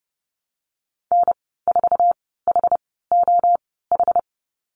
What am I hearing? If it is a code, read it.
Morse code, heard as a keyed tone, about 30 wpm, spaced as for 17 wpm, 700 Hz. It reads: N4HOH
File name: Lesson 15g - Number 5 - Call Signs 30wpm